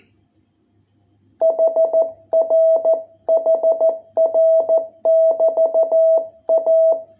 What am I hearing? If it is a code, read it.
Morse code, heard as a keyed tone, about 14 wpm, 660 Hz. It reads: HR HR <BT> A